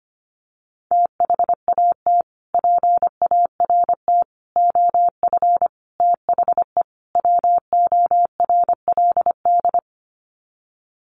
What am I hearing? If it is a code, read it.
Morse code, heard as a keyed tone, about 25 wpm, 700 Hz. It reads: THAT PART OF THE WORLD
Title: Sets of 5 Words Encore2 25wpm